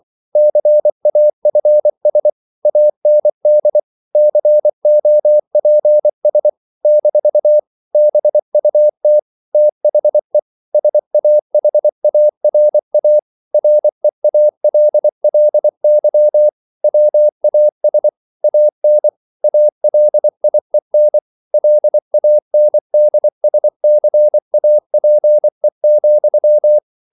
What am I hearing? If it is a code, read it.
Morse code, heard as a keyed tone, about 24 wpm, 600 Hz. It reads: CAFS AND COPS - BUT THE SAHARA REALLY WAS AN ALIEN LANDSCAPE,